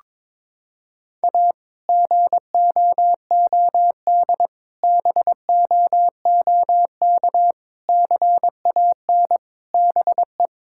Morse code, heard as a keyed tone, about 22 wpm, 700 Hz. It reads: A GOOD BOOK CAN BE
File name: practice / 22wpm / Sentences with 2-Word Phrases - 22wpm